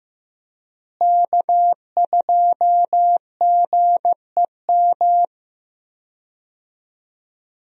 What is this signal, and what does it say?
Morse code, heard as a keyed tone, about 15 wpm, 700 Hz. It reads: K2GEM
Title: USA Call Signs - 5 Characters - Repeated 2x - 15wpm